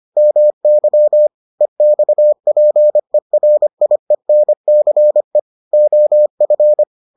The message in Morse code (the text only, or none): MY EXPERIENCE OF